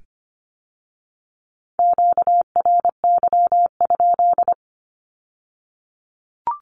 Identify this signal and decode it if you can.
Morse code, heard as a keyed tone, about 25 wpm, 700 Hz. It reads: QRY?
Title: Q-Codes with Spoken Meaning - 25wpm